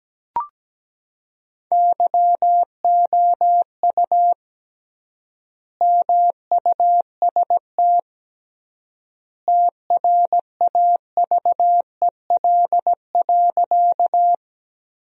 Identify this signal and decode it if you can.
Morse code, heard as a keyed tone, about 17 wpm, 700 Hz. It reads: YOU MUST TRAVEL.
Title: 3x Word-Spacing - Sentences from Top 400 Words - 17wpm